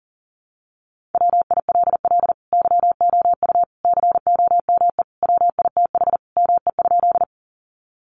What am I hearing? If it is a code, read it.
Morse code, heard as a keyed tone, about 40 wpm, 700 Hz. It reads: WILL YOU COME WITH ME?
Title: Sentences from Top 200 Words 40wpm